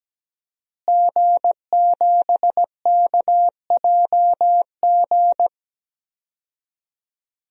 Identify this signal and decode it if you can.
Morse code, heard as a keyed tone, about 17 wpm, 700 Hz. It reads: G7KJG
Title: DXCC-100 Call Signs - 5 Characters - Repeated 3x - 17wpm